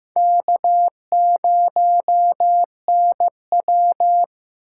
Morse code, heard as a keyed tone, about 16 wpm, 700 Hz. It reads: K0NW
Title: K1USN SST QSO Practice - 15wpm